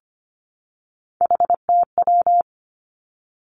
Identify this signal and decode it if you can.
Morse code, heard as a keyed tone, about 25 wpm, 700 Hz. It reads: HTW
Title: Three Challenging Letter-Number Combinations 25wpm